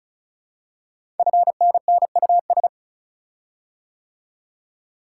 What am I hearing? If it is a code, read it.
Morse code, heard as a keyed tone, about 36 wpm, 690 Hz. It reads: FNNUS
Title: Code Groups - 5 Letters - 35wpm